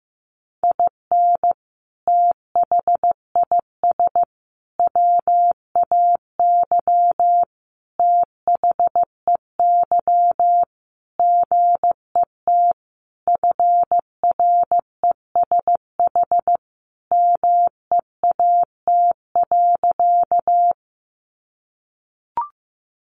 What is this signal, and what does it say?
Morse code, heard as a keyed tone, about 15 wpm, 700 Hz. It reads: IN THIS WAY THEY GET FRESH MEAT.
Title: Sentences from Top 1200 Words 15wpm